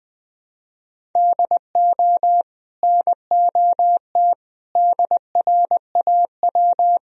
Morse code, heard as a keyed tone, about 20 wpm, 700 Hz. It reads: DO NOT DRAW
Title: Sentences from Top 1000 Words - Review Part 1 - 20wpm